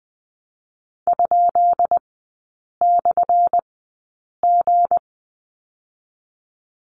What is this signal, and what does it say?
Morse code, heard as a keyed tone, about 20 wpm, spaced as for 8 wpm, 700 Hz. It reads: ?/G
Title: 205 Lesson 33e - Letter j - Three Characters 20wpm